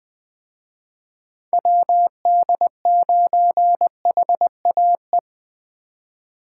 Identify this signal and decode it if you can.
Morse code, heard as a keyed tone, about 20 wpm, 700 Hz. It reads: WD9HAE
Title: Speed Racing - USA 2x3 Call Signs 17wpm